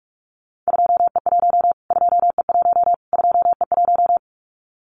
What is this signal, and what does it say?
Morse code, heard as a keyed tone, about 44 wpm, 700 Hz. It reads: 2E1 2E1 2E1
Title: Callsign Prefixes for Top 25 DXCC - Repeated 3x - 45wpm